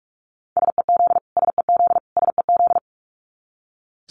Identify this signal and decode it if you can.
Morse code, heard as a keyed tone, about 44 wpm, 700 Hz. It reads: SEZ SEZ SEZ
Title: Abbreviations Repeated 3x - 45wpm